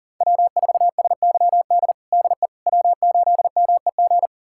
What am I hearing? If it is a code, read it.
Morse code, heard as a keyed tone, about 40 wpm, 700 Hz. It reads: W4SYD DE W8MEG
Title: USA Callsign DE Callsign - 5 Characters - 40wpm